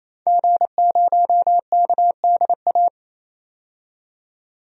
Morse code, heard as a keyed tone, about 28 wpm, 700 Hz. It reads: G0KDA
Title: DXCC-100 Call Signs - 5 Characters - 28wpm